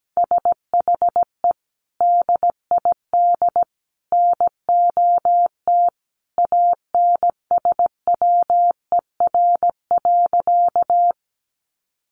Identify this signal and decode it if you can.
Morse code, heard as a keyed tone, about 17 wpm, 700 Hz. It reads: SHE DID NOT ANSWER.
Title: Sentences from Top 600 Words 17wpm